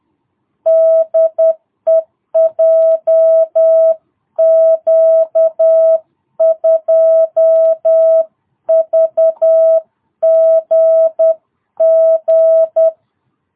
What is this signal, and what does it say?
Morse code, heard as a keyed tone, about 10 wpm, 660 Hz. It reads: DE JQ2VGG
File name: jh8cbh / 2022cw